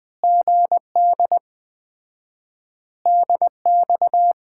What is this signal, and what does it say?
Morse code, heard as a keyed tone, about 20 wpm, 700 Hz. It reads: GD DX
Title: Common QSO Phrases - 4x Word-Spacing - 20wpm